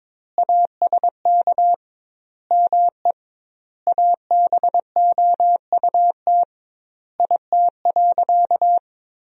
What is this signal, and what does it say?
Morse code, heard as a keyed tone, about 22 wpm, 700 Hz. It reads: ASK ME ABOUT IT.
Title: 2x Word-Spacing - Sentences from Top 200 Words - 22wpm